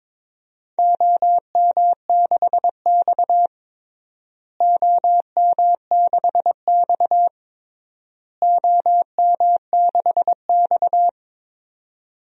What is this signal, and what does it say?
Morse code, heard as a keyed tone, about 22 wpm, 700 Hz. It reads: OM6X OM6X OM6X
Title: DXCC-100 Call Signs - 4 Characters - Repeated 3x - 22wpm